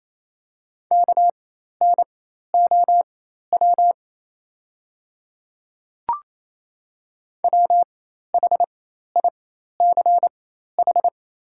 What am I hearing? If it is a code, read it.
Morse code, heard as a keyed tone, about 28 wpm, spaced as for 13 wpm, 690 Hz. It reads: KNOW WHICH
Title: 4x Character Spacing - Top 200 Words - 28wpm